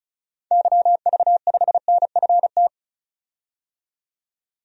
Morse code, heard as a keyed tone, about 36 wpm, 700 Hz. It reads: YV5NFT
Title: DXCC-100 Call Signs - 6 Characters - 35wpm